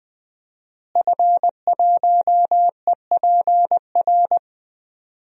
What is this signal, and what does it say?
Morse code, heard as a keyed tone, about 20 wpm, 700 Hz. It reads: F1EPR
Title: DXCC-100 Call Signs - 5 Characters - Repeated 3x - 20wpm